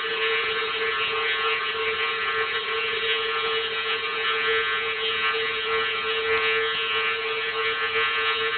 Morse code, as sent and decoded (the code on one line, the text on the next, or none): none